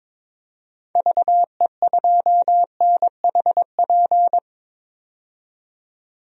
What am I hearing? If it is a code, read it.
Morse code, heard as a keyed tone, about 22 wpm, 700 Hz. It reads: VE2NHP
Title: DXCC-100 Call Signs - 6 Characters - 22wpm